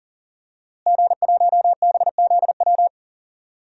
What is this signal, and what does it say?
Morse code, heard as a keyed tone, about 40 wpm, 700 Hz. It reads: G1BZW
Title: DXCC-100 Call Signs - 5 Characters - Repeated 3x - 40wpm